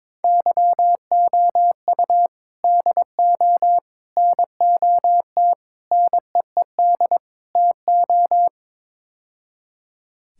Morse code, heard as a keyed tone, about 22 wpm, 700 Hz. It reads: YOU DO NOT NEED TO
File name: practice / 22wpm / Build A Phrase - 5-Word Phrases - 22wpm